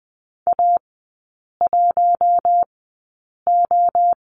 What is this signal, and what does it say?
Morse code, heard as a keyed tone, about 20 wpm, spaced as for 8 wpm, 700 Hz. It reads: A1O
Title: 041 Lesson 8e - Number 1 - Three Characters 20wpm